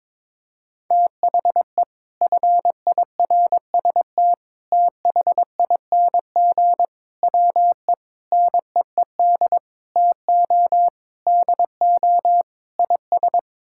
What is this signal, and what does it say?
Morse code, heard as a keyed tone, about 22 wpm, 700 Hz. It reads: THE FIRST THING WE NEED TO DO IS
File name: practice / 22wpm / Sets of 8 Words - 22wpm